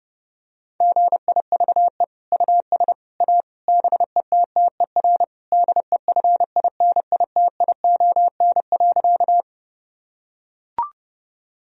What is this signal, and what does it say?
Morse code, heard as a keyed tone, about 30 wpm, 700 Hz. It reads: GIVE US A BETTER DEFINITION.
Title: Sentences from Top 1000 Words - Review Part 4 - 30wpm